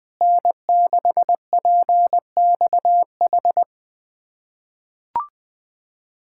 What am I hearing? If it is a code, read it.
Morse code, heard as a keyed tone, about 20 wpm, 700 Hz. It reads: N6PXH
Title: USA Callsign DE Callsign - 5 Characters - 20wpm